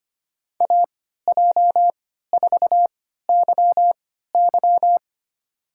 Morse code, heard as a keyed tone, about 25 wpm, spaced as for 14 wpm, 700 Hz. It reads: AJ4YY
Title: USA Call Signs - 5 Characters - 3x Character Spacing - 25wpm